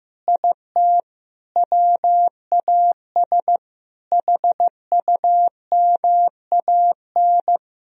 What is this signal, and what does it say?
Morse code, heard as a keyed tone, about 15 wpm, 700 Hz. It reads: IT WAS HUMAN
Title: Sentences from Top 1000 Words - Review Part 3 - 15wpm